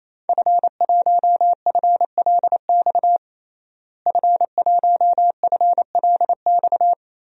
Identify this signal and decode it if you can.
Morse code, heard as a keyed tone, about 28 wpm, 700 Hz. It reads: F1FLX F1FLX
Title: DXCC-100 Call Signs - 5 Characters - Repeated 3x - 28wpm